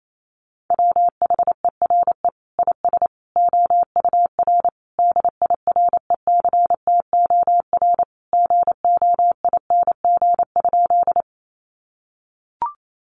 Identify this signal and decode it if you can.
Morse code, heard as a keyed tone, about 28 wpm, 700 Hz. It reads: WHERE IS OUR DIRECTOR GOING?
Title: Sentences from Top 1300 Words 28wpm